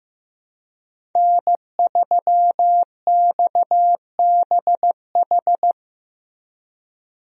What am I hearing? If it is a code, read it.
Morse code, heard as a keyed tone, about 15 wpm, 700 Hz. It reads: N3XBH
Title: USA Call Signs - 5 Characters - Repeated 2x - 15wpm